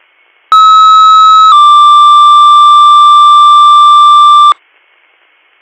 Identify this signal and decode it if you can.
Two-tone page, heard as a one-second tone, then a three-second tone, about 1290 then 1180 Hz. It is heard from 0.5 to 4.5 s.